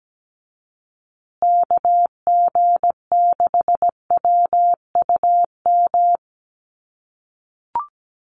Morse code, heard as a keyed tone, about 17 wpm, 700 Hz. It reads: KG6WUM
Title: USA Call Signs Repeated 3x 17wpm